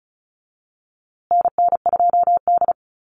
Morse code, heard as a keyed tone, about 34 wpm, 700 Hz. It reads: NN2D
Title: USA Call Signs Repeated 2x 35wpm